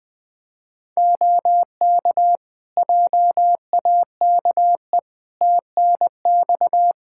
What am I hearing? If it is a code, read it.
Morse code, heard as a keyed tone, about 20 wpm, 700 Hz. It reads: OK JAKE TNX
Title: Practice QSO - Exchange 4 - Ending - 20wpm